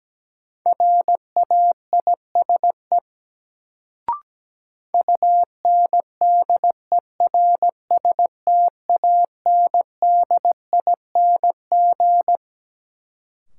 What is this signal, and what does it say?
Morse code, heard as a keyed tone, about 17 wpm, 700 Hz. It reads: RAISE UNDERSTANDING
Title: Sentences from Top 1400 Words - New Word Drill Alt - 17wpm